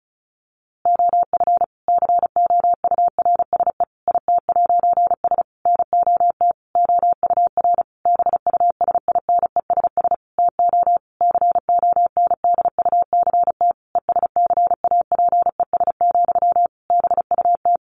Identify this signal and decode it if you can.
Morse code, heard as a keyed tone, about 34 wpm, 700 Hz. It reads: OF COURSE IT'S NOT OUR BUSINESS TO CONDUCT ESCAPES, BUT
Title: A Sound Of Thunder 35wpm